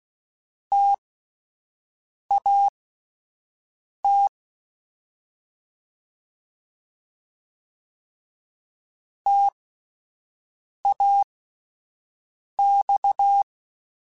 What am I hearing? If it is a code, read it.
Morse code, heard as a keyed tone, about 16 wpm, spaced as for 5 wpm, 800 Hz. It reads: TAT TAX
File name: Alphabetical / Alphabetical mp3s / Ts @5